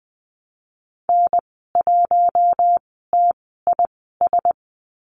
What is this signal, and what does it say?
Morse code, heard as a keyed tone, about 20 wpm, spaced as for 14 wpm, 700 Hz. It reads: N1TIS
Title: 042 Lesson 8g - Number 1 - Call Signs 20wpm